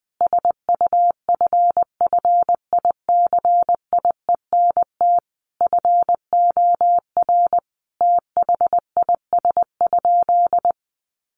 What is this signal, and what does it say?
Morse code, heard as a keyed tone, about 20 wpm, 700 Hz. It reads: SUFFICIENT FOR THIS?
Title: Sentences from Top 1400 Words 20wpm